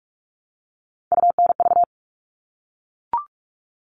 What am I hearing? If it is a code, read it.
Morse code, heard as a keyed tone, about 44 wpm, 700 Hz. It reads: UNV
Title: All Cut Numbers - 3-Digit 45wpm